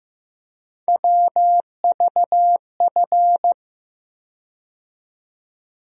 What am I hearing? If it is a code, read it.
Morse code, heard as a keyed tone, about 15 wpm, 700 Hz. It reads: WVF
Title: USA Call Sign Postfixes - 3 Letters 15wpm